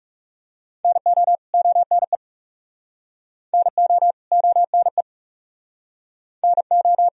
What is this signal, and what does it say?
Morse code, heard as a keyed tone, about 40 wpm, 700 Hz. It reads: NO ONE NO ONE NO